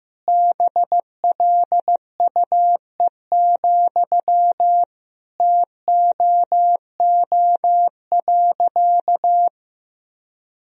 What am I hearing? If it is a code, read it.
Morse code, heard as a keyed tone, about 15 wpm, 700 Hz. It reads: BLUE, TOO.
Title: Sentences from Top 1000 Words - Review Part 3 - 15wpm